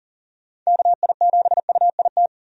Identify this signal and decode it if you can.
Morse code, heard as a keyed tone, about 39 wpm, 700 Hz. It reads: KI7UIT
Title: USA Call Signs - 6 Characters - 40wpm